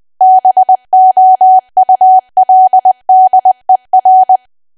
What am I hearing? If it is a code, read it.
Morse code, heard as a keyed tone, about 20 wpm, 750 Hz. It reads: BOULDER
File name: Archive / 20 WPM / 20WPM